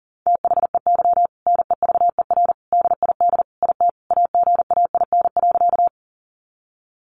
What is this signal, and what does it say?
Morse code, heard as a keyed tone, about 40 wpm, 700 Hz. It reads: THEY NEVER DID IT AGAIN.
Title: Sentences from Top 300 Words 40wpm